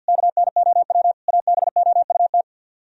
Morse code, heard as a keyed tone, about 64 wpm, 700 Hz. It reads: KNOW ABOUT